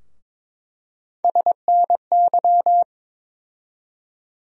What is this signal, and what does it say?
Morse code, heard as a keyed tone, about 22 wpm, 700 Hz. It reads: SNY
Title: USA Call Sign Postfixes - 3 Letters 22wpm